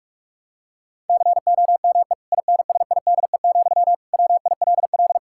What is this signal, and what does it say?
Morse code, heard as a keyed tone, about 47 wpm, 700 Hz. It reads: COME INSIDE, WILL
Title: Sentences from Top 1000 Words - Review Part 6 - 45wpm